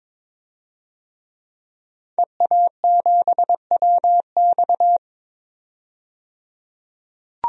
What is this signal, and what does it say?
Morse code, heard as a keyed tone, about 22 wpm, 700 Hz. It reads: EA7WX
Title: DXCC-100 Call Signs - 5 Characters - 22wpm